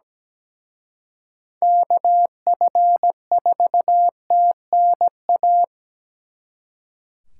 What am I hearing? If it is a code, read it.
Morse code, heard as a keyed tone, about 20 wpm, 700 Hz. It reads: KF4TNA
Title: Speed Racing - USA 2x3 Call Signs 17wpm